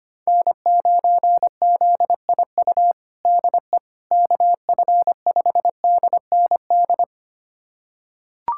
Morse code, heard as a keyed tone, about 25 wpm, 700 Hz. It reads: N9ZIU DE KF5DND
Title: USA Callsign DE Callsign 25wpm